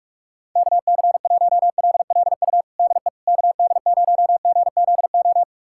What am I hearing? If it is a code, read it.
Morse code, heard as a keyed tone, about 42 wpm, 700 Hz. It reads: KC1LLU DE KD0GZO
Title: USA Callsign DE Callsign 45wpm